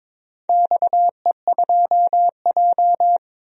Morse code, heard as a keyed tone, about 22 wpm, 700 Hz. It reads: XE2J
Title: DXCC-100 Call Signs - 4 Characters - 22wpm